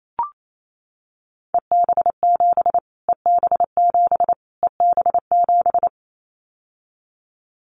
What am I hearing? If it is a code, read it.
Morse code, heard as a keyed tone, about 28 wpm, 700 Hz. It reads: EB7 EB7 EB7
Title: Callsign Prefixes for Top 25 DXCC - Repeated 3x - 28wpm